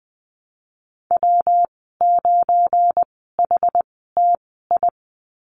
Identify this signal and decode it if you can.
Morse code, heard as a keyed tone, about 20 wpm, spaced as for 14 wpm, 700 Hz. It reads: W9HTI
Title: 193 Lesson 31g - Letter v - Call Signs 20wpm